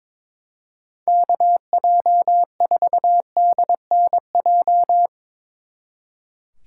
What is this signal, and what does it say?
Morse code, heard as a keyed tone, about 25 wpm, 700 Hz. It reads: KJ4DNJ
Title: Speed Racing - USA 2x3 Call Signs 22wpm